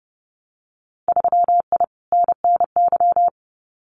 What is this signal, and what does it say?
Morse code, heard as a keyed tone, about 30 wpm, 700 Hz. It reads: 3I NNY
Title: Winter Field Day Exchanges 30wpm